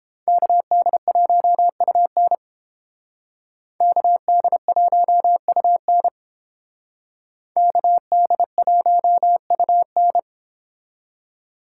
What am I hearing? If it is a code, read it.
Morse code, heard as a keyed tone, about 31 wpm, 700 Hz. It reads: KD1UN KD1UN KD1UN